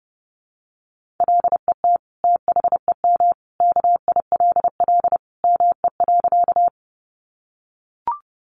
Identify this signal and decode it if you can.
Morse code, heard as a keyed tone, about 30 wpm, 700 Hz. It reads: LET THEM KILL ME.
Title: Sentences from Top 1400 Words 30wpm